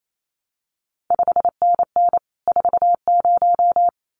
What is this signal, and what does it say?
Morse code, heard as a keyed tone, about 28 wpm, 700 Hz. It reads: 5NN 40
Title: CQ WW DX Contest Exchange Encore - 28wpm